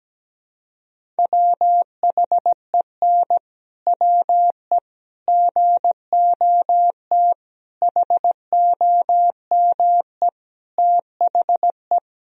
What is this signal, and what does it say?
Morse code, heard as a keyed tone, about 17 wpm, 700 Hz. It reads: WHEN WE GOT HOME THE
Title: Sentences from Top 500 Words and Abbreviations 17wpm